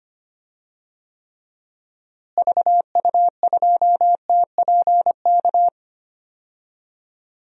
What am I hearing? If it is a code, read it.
Morse code, heard as a keyed tone, about 25 wpm, 700 Hz. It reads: VU2TPK
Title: DXCC-100 Call Signs - 6 Characters - Repeated 3x - 25wpm